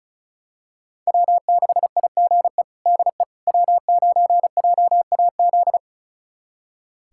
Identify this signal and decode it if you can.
Morse code, heard as a keyed tone, about 36 wpm, 700 Hz. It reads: W6IGE DE W9JAZ
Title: USA Callsign DE Callsign - 5 Characters - 35wpm